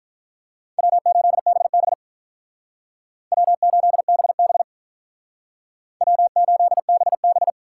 Morse code, heard as a keyed tone, about 43 wpm, 700 Hz. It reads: W8BB W8BB W8BB
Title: Speed Racing - USA Call Signs 35wpm